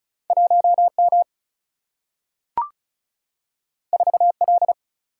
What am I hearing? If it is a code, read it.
Morse code, heard as a keyed tone, about 36 wpm, 700 Hz. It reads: 1 M 4 L
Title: Single-Letter Single-Number 35wpm